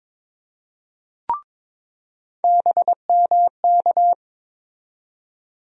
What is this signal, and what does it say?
Morse code, heard as a keyed tone, about 22 wpm, 700 Hz. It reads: BMK
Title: USA Call Sign Postfixes - 3 Letters 22wpm